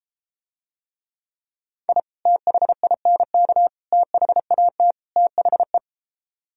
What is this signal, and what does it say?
Morse code, heard as a keyed tone, about 33 wpm, 700 Hz. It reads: I THINK THAT THE